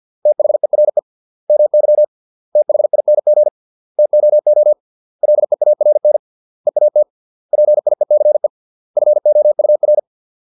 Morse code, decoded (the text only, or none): THERE MY THING TOO LEARN EAT PIECE FOUR